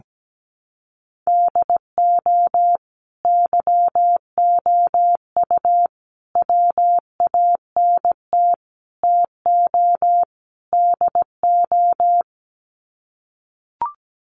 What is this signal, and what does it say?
Morse code, heard as a keyed tone, about 17 wpm, 700 Hz. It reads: DO YOU WANT TO DO
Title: Sets of 5 Words Encore 17wpm